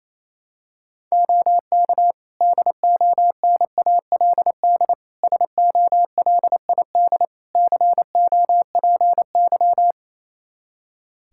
Morse code, heard as a keyed tone, about 28 wpm, 700 Hz. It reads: OK DONALD SOLID COPY
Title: Practice QSO - Exchange 3 - More Info - 28wpm